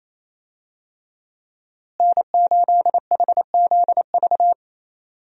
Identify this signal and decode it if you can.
Morse code, heard as a keyed tone, about 28 wpm, 700 Hz. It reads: N8HZV
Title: USA Call Signs - 5 Characters - Repeated 3x - 28wpm